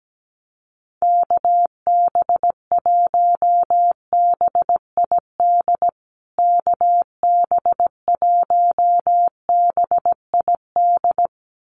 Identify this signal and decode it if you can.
Morse code, heard as a keyed tone, about 17 wpm, 700 Hz. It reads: KB1BID KB1BID
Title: USA Call Signs Repeated 2x 17wpm